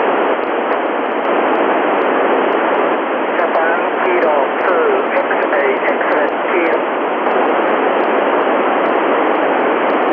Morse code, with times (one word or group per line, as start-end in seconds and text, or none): none